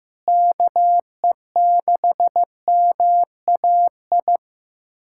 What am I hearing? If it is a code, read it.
Morse code, heard as a keyed tone, about 15 wpm, 700 Hz. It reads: KE6MAI
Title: USA Callsign DE Callsign 15wpm